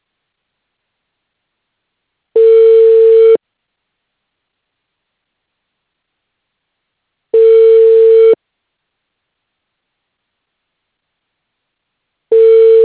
Ringback tone (China, 450 Hz).